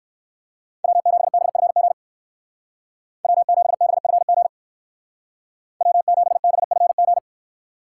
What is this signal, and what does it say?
Morse code, heard as a keyed tone, about 59 wpm, 700 Hz. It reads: W7BFG W7BFG W7BFG